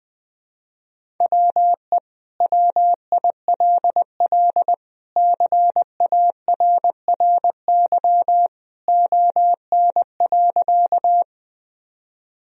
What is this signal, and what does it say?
Morse code, heard as a keyed tone, about 20 wpm, 700 Hz. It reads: WE WILL CARRY ON.
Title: Sentences from Top 1000 Words - Review Part 1 - 20wpm